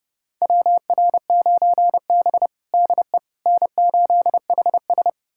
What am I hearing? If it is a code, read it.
Morse code, heard as a keyed tone, about 30 wpm, 700 Hz. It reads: WR9B DE N8HS
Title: USA Callsign DE Callsign - 4 Characters - 30wpm